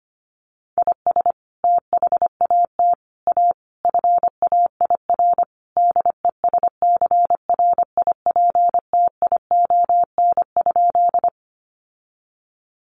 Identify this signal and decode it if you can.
Morse code, heard as a keyed tone, about 25 wpm, 700 Hz. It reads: IS THAT A FAIR DESCRIPTION?
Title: Sentences from Top 1300 Words 25wpm